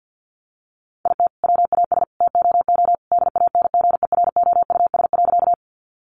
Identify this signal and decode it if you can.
Morse code, heard as a keyed tone, about 50 wpm, 700 Hz. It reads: IT WAS TOO DANGEROUS.
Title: Sentences from Top 1200 Words 50wpm